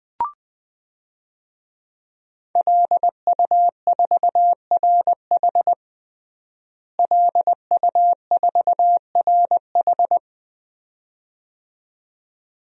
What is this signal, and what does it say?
Morse code, heard as a keyed tone, about 20 wpm, 700 Hz. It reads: LU4RH LU4RH
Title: DXCC-100 Call Signs - 5 Characters - Repeated 2x - 20wpm